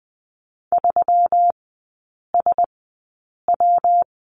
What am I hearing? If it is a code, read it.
Morse code, heard as a keyed tone, about 20 wpm, spaced as for 8 wpm, 700 Hz. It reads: 3SW
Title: 152 Lesson 25e - Letter p - Three Characters 20wpm